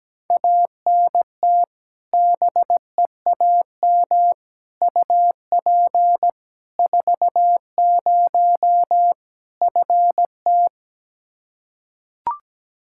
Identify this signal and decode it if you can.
Morse code, heard as a keyed tone, about 17 wpm, 700 Hz. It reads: ANT BEAM UP 40 FT
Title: Practice QSO - Exchange 2 - Info - 17wpm